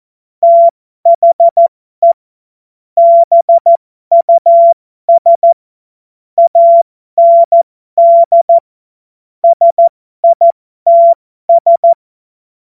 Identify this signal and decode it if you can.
Morse code, heard as a keyed tone, about 14 wpm, spaced as for 12 wpm, 680 Hz. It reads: THE BUS AND SITS
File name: Morsekurs / Witze